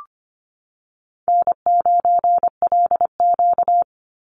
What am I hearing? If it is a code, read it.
Morse code, heard as a keyed tone, about 25 wpm, 700 Hz. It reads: N9LQ
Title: Easy Call Signs - 25wpm